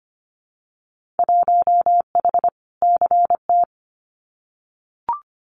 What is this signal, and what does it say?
Morse code, heard as a keyed tone, about 25 wpm, 700 Hz. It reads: 1H CT
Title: Winter Field Day Exchanges 25wpm